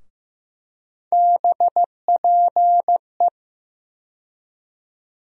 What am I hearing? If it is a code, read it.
Morse code, heard as a keyed tone, about 15 wpm, 700 Hz. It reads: BPE
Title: USA Call Sign Postfixes - 3 Letters 15wpm